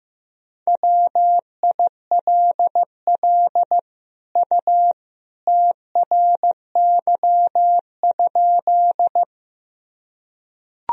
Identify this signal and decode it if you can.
Morse code, heard as a keyed tone, about 15 wpm, 700 Hz. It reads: WILL U TRY?
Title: Sentences from Top 200 Words and Abbreviations 15wpm